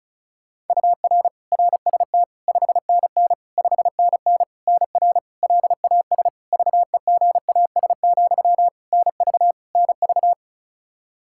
Morse code, heard as a keyed tone, about 34 wpm, 700 Hz. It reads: UR RST 5NN 5NN NR LAS VEGAS, NV NV
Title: K1USN SST QSO Practice 35wpm